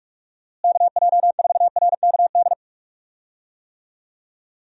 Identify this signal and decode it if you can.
Morse code, heard as a keyed tone, about 46 wpm, 700 Hz. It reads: KJ4RKD
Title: USA Call Signs - 6 Characters - Repeated 3x - 45wpm